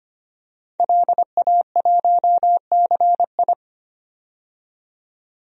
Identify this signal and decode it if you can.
Morse code, heard as a keyed tone, about 25 wpm, 700 Hz. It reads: LA1CI
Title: DXCC-100 Call Signs - 5 Characters - 25wpm